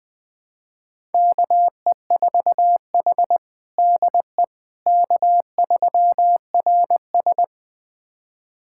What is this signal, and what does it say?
Morse code, heard as a keyed tone, about 20 wpm, 700 Hz. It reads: KE4H DE K3RS
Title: USA Callsign DE Callsign - 4 Characters - 20wpm